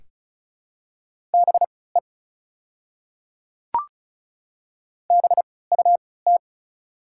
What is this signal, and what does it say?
Morse code, heard as a keyed tone, about 34 wpm, spaced as for 20 wpm, 690 Hz. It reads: BE BUT
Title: 3x Character Spacing - Top 100 Words - 35wpm